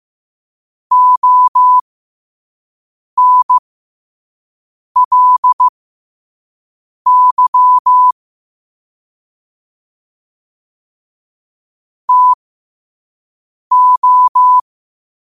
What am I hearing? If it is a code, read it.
Morse code, heard as a keyed tone, about 15 wpm, spaced as for 5 wpm, 1000 Hz. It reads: ONLY TO
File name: qotdfiles / 5wpm